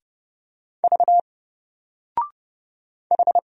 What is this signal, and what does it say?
Morse code, heard as a keyed tone, about 30 wpm, 700 Hz. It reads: V H
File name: Single-Letter-Number-30wpm